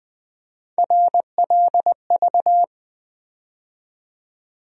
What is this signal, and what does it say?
Morse code, heard as a keyed tone, about 20 wpm, 700 Hz. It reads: RLV
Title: USA Call Sign Postfixes - 3 Letters 20wpm